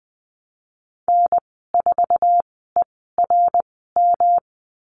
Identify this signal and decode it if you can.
Morse code, heard as a keyed tone, about 20 wpm, spaced as for 15 wpm, 700 Hz. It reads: N4ERM
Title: 154 Lesson 25g - Letter p - Call Signs 20wpm